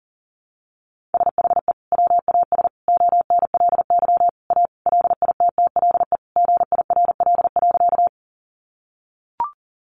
Morse code, heard as a keyed tone, about 40 wpm, 700 Hz. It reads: SHE WAS ONLY A LITTLE GIRL.
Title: Sentences from Top 400 Words 40wpm